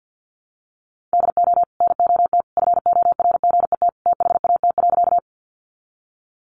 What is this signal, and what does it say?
Morse code, heard as a keyed tone, about 50 wpm, 700 Hz. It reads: DO NOT FORGET THAT.
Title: Sentences from Top 1200 Words 50wpm